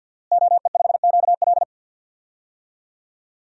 Morse code, heard as a keyed tone, about 51 wpm, 700 Hz. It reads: OE5QL
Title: DXCC-100 Call Signs - 5 Characters - Repeated 2x - 50wpm